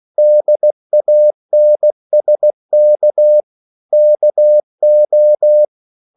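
Morse code, heard as a keyed tone, about 16 wpm, 600 Hz. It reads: DANSK KO
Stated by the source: Slow (QRS) Morse code at 16 wpm, from CW med Gnister.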